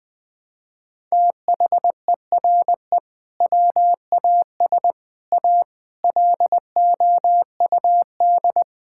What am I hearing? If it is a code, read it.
Morse code, heard as a keyed tone, about 20 wpm, 700 Hz. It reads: THERE WAS A LOUD
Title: Sentences with 3-Word Phrases - Spoken Beginning - 20wpm